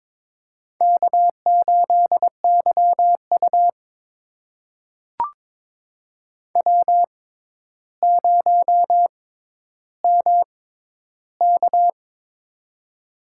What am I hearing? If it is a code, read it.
Morse code, heard as a keyed tone, about 22 wpm, spaced as for 11 wpm, 700 Hz. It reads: K8YU W0MK
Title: USA Call Signs - 4 Characters - 6x Character Spacing - 22wpm